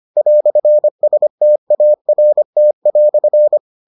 Morse code, heard as a keyed tone, about 25 wpm, 600 Hz. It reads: "START"